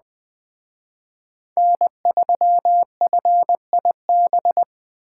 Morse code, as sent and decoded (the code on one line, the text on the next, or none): -. ...-- ..-. .. -...
N3FIB